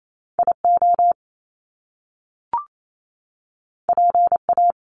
Morse code, heard as a keyed tone, about 28 wpm, 700 Hz. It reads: IO PA
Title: Top Two Letter Combinations in Words 28wpm